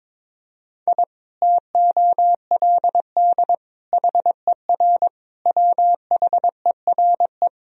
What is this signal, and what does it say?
Morse code, heard as a keyed tone, about 22 wpm, 700 Hz. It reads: I TOLD HER WHERE
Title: Sentences from Top 1000 Words - Review Part 2 - 22wpm